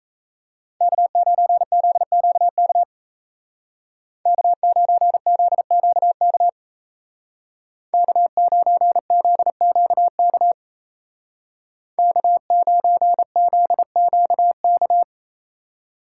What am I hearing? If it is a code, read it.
Morse code, heard as a keyed tone, about 33 wpm, 700 Hz. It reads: K9ZQK K9ZQK K9ZQK K9ZQK
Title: Speed Racing - USA 1x3 Call Signs 28wpm